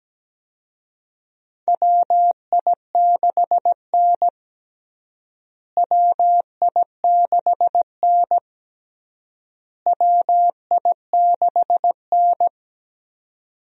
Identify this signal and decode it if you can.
Morse code, heard as a keyed tone, about 17 wpm, 700 Hz. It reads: WI6N WI6N WI6N
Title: USA Call Signs - 4 Characters - Repeated 3x - 17wpm